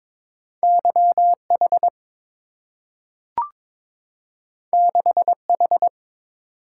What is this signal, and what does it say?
Morse code, heard as a keyed tone, about 22 wpm, 700 Hz. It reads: YH 6H
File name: practice / 22wpm / Two Letter-Number Combinations - 22wpm